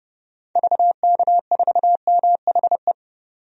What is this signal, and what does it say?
Morse code, heard as a keyed tone, about 30 wpm, 700 Hz. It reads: VK4MHE
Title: DXCC-100 Call Signs - 6 Characters - 30wpm